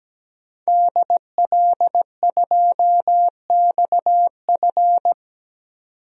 Morse code, heard as a keyed tone, about 17 wpm, 700 Hz. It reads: DL2XF
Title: DXCC-100 Call Signs - 5 Characters - 17wpm